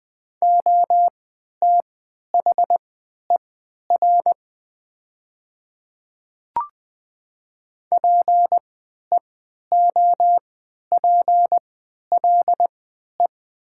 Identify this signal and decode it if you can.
Morse code, heard as a keyed tone, about 20 wpm, spaced as for 11 wpm, 700 Hz. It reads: OTHER PEOPLE
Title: 3x Character Spacing - Top 100 Words - 20wpm